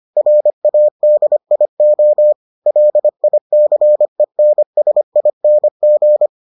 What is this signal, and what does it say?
Morse code, heard as a keyed tone, about 25 wpm, 600 Hz. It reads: RADIO LICENSING